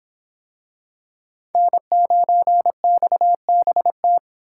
Morse code, heard as a keyed tone, about 24 wpm, 700 Hz. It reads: N9XBT